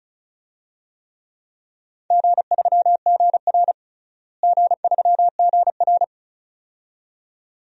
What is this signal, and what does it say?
Morse code, heard as a keyed tone, about 36 wpm, 700 Hz. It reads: G3GR G3GR
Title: DXCC-100 Call Signs - 4 Characters - Repeated 2x - 35wpm